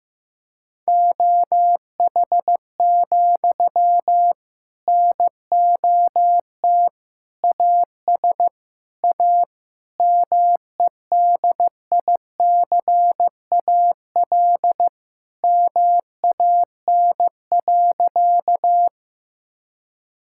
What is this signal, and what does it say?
Morse code, heard as a keyed tone, about 15 wpm, 700 Hz. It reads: OH, NOT AS A MEDICAL MAN.
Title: Sentences from Top 1000 Words - Review Part 5 - 15wpm